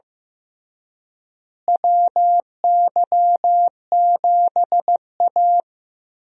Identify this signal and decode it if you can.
Morse code, heard as a keyed tone, about 15 wpm, 700 Hz. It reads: WY7A
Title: USA Call Signs - 4 Characters - Repeated 3x - 15wpm